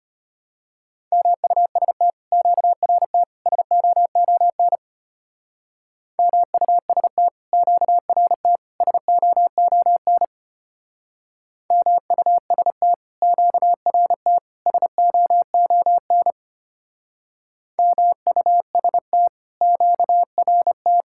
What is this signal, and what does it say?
Morse code, heard as a keyed tone, about 33 wpm, 700 Hz. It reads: MUST QRT SOON MUST QRT SOON MUST QRT SOON MUST QRT